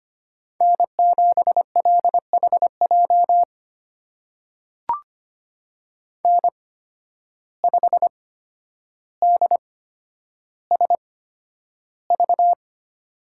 Morse code, heard as a keyed tone, about 25 wpm, spaced as for 11 wpm, 700 Hz. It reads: N7LHJ N5DSV
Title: USA Call Signs - 5 Characters - 8x Character Spacing - 25wpm